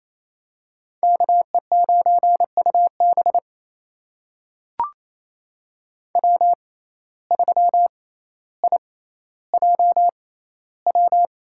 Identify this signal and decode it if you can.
Morse code, heard as a keyed tone, about 28 wpm, spaced as for 14 wpm, 700 Hz. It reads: KE9UB W3IJW
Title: USA Call Signs - 5 Characters - 6x Character Spacing - 28wpm